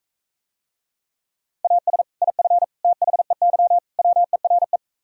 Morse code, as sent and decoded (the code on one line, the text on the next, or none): .- ... / .. ..-. / - .... -.-- / .-- . .-. .
AS IF THEY WERE